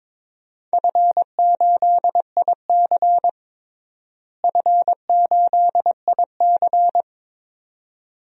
Morse code, heard as a keyed tone, about 22 wpm, 700 Hz. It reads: F8IC F8IC
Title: DXCC-100 Call Signs - 4 Characters - Repeated 2x - 22wpm